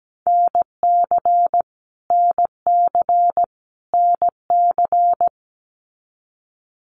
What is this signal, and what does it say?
Morse code, heard as a keyed tone, about 17 wpm, 700 Hz. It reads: NC NC NC
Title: US States Repeated 3x 17wpm